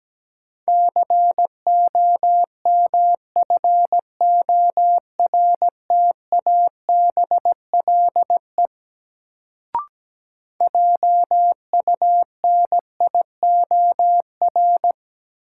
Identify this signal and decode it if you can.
Morse code, heard as a keyed tone, about 17 wpm, 700 Hz. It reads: COMFORTABLE JUNIOR
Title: Sentences from Top 1500 Words - New Word Drill Alt - 17wpm